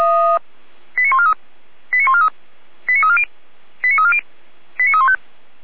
Five-tone selective call (ZVEI), heard as quick runs of five tones, tones of about 70 ms. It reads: 89143 89143 89340 89340 89315